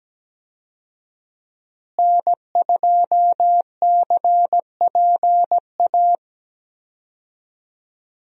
Morse code, heard as a keyed tone, about 17 wpm, 700 Hz. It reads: N2CPA
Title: USA Call Signs - 5 Characters - Repeated 3x - 17wpm